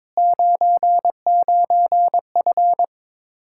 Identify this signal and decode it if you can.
Morse code, heard as a keyed tone, about 22 wpm, 700 Hz. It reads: 99F
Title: Speed-Racing Common QSO Phrases - 22wpm